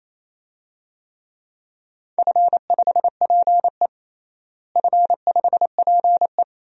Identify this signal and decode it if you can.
Morse code, heard as a keyed tone, about 28 wpm, 700 Hz. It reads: F5PE F5PE
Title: DXCC-100 Call Signs - 4 Characters - Repeated 2x - 28wpm